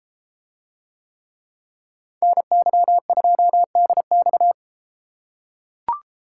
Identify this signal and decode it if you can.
Morse code, heard as a keyed tone, about 32 wpm, 700 Hz. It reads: NY2DX